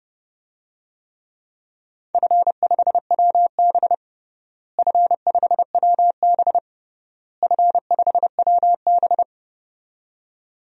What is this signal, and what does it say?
Morse code, heard as a keyed tone, about 30 wpm, 700 Hz. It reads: F5WB F5WB F5WB
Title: DXCC-100 Call Signs - 4 Characters - Repeated 3x - 30wpm